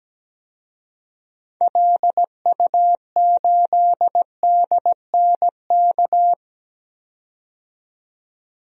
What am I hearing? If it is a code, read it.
Morse code, heard as a keyed tone, about 17 wpm, 700 Hz. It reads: LU8DNK
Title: DXCC-100 Call Signs - 6 Characters - 17wpm